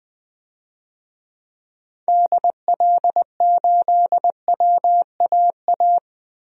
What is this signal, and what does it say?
Morse code, heard as a keyed tone, about 20 wpm, 700 Hz. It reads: DL8WAA
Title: DXCC-100 Call Signs - 6 Characters - Repeated 2x - 20wpm